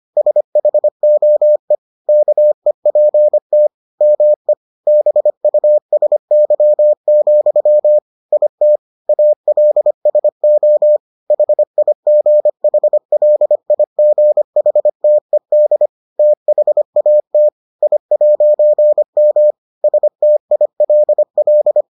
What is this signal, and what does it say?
Morse code, heard as a keyed tone, about 25 wpm, 600 Hz. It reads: SHOE KEPT ME BUSY, IT ALSO HIGHLIGHTED THAT I'M STILL